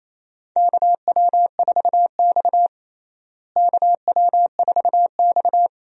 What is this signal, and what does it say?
Morse code, heard as a keyed tone, about 28 wpm, 700 Hz. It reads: KW4X KW4X
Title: USA Call Signs - 4 Characters - Repeated 2x - 28wpm